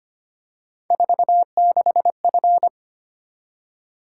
Morse code, heard as a keyed tone, about 25 wpm, 700 Hz. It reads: 46F
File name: Three Letter-Number Combinations - 25wpm